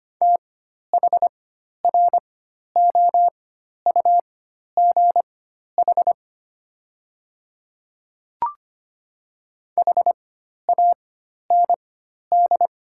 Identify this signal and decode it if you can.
Morse code, heard as a keyed tone, about 25 wpm, spaced as for 12 wpm, 700 Hz. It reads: THROUGH HAND
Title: 4x Character Spacing - Top 200 Words - 25wpm